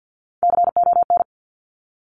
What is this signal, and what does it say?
Morse code, heard as a keyed tone, about 50 wpm, 700 Hz. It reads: CON
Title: Top Three Letter Combinations in Words 50wpm